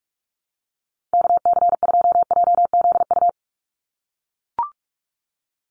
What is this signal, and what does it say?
Morse code, heard as a keyed tone, about 44 wpm, 700 Hz. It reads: KC2JZU
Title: USA Call Signs 45wpm